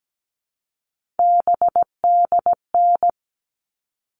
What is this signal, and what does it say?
Morse code, heard as a keyed tone, about 17 wpm, 700 Hz. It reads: BDN
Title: All Cut Numbers - 3-Digit 17wpm